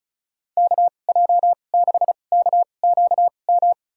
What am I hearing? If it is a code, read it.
Morse code, heard as a keyed tone, about 34 wpm, spaced as for 25 wpm, 700 Hz. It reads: KJ6KQM
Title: USA Call Signs - 6 Characters - 2x Character Spacing - 35wpm